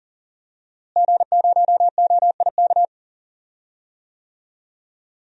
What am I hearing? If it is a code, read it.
Morse code, heard as a keyed tone, about 40 wpm, 700 Hz. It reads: G0OIK
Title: DXCC-100 Call Signs - 5 Characters - Repeated 2x - 40wpm